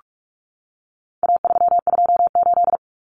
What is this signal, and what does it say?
Morse code, heard as a keyed tone, about 44 wpm, 700 Hz. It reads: A328
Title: Common Cut Numbers - 4-Digit 45wpm